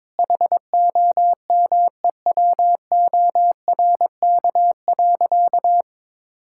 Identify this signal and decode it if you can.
Morse code, heard as a keyed tone, about 22 wpm, 700 Hz. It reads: HOMEWORK.
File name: practice / 22wpm / Sentences with 4-Word Phrases - 22wpm